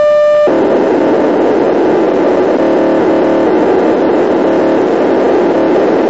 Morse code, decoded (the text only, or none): none